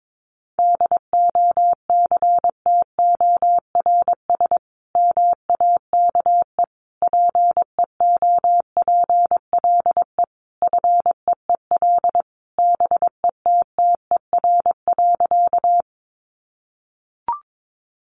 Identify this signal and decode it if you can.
Morse code, heard as a keyed tone, about 22 wpm, 700 Hz. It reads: DOCTORS MAKE PEOPLE FEEL BETTER.
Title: Easy Phrases 22wpm